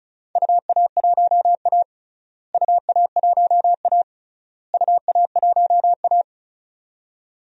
Morse code, heard as a keyed tone, about 36 wpm, 700 Hz. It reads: UA1A UA1A UA1A
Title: DXCC-100 Call Signs - 4 Characters - Repeated 3x - 35wpm